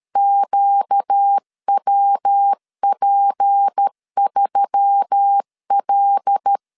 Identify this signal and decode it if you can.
Morse code, heard as a keyed tone, about 13 wpm, 800 Hz. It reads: QWP3L
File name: CW / DET / 1 6 1500C S A 12wpm 3440